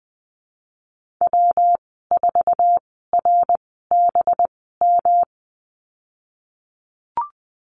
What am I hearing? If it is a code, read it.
Morse code, heard as a keyed tone, about 20 wpm, spaced as for 15 wpm, 700 Hz. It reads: W4RBM
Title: 200 Lesson 32g - Letter k - Call Signs 20wpm